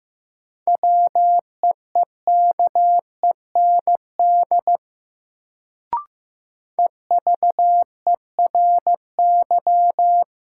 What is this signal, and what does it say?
Morse code, heard as a keyed tone, about 15 wpm, 700 Hz. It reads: WEEKEND EVERY
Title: Sets of 5 Words - New Word Drill - 15wpm